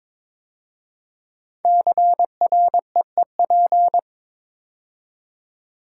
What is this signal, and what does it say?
Morse code, heard as a keyed tone, about 23 wpm, 700 Hz. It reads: CREEP
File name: 5 Letter Words - First 3 Letters Given - 22wpm